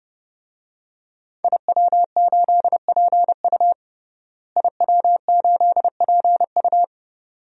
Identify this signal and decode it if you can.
Morse code, heard as a keyed tone, about 30 wpm, 700 Hz. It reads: IW8PU IW8PU
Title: DXCC-100 Call Signs - 5 Characters - Repeated 2x - 30wpm